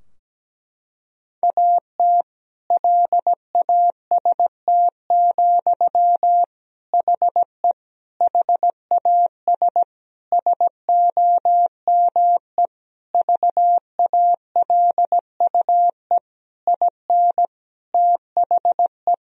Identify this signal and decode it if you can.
Morse code, heard as a keyed tone, about 17 wpm, 700 Hz. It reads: AT LAST, HE HAS SOME VALUE IN THE
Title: Sentences from Top 1000 Words - Review Part 3 - 17wpm